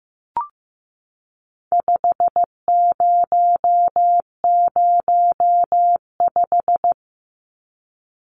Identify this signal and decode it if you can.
Morse code, heard as a keyed tone, about 15 wpm, 700 Hz. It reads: 5005
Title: Numbers - 4-Digit 15wpm